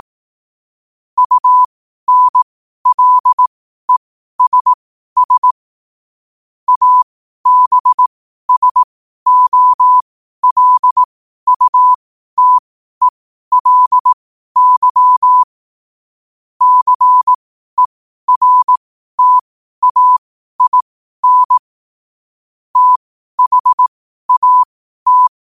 Morse code, heard as a keyed tone, about 18 wpm, spaced as for 12 wpm, 1000 Hz. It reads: UNLESS ABSOLUTELY CERTAIN THAT
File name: qotdfiles / 12wpm